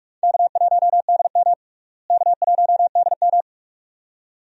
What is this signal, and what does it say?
Morse code, heard as a keyed tone, about 44 wpm, 700 Hz. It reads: K1DM K1DM
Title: USA Call Signs - 4 Characters - Repeated 2x - 45wpm